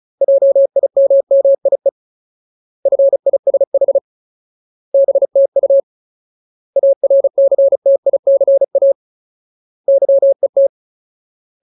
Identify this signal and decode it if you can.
Morse code, heard as a keyed tone, about 35 wpm, 550 Hz. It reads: JIMMIE FISH BTU ARCTICA YET
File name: Ultimate Word List / 35WPM Ultimate